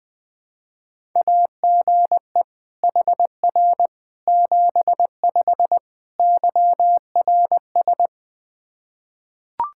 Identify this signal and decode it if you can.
Morse code, heard as a keyed tone, about 20 wpm, 700 Hz. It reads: AGE HR 75 YRS
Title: K1USN SST QSO Practice 20wpm